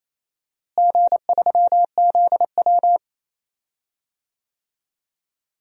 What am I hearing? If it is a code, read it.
Morse code, heard as a keyed tone, about 28 wpm, 700 Hz. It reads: G3ZW
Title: DXCC-100 Call Signs - 4 Characters - Repeated 3x - 28wpm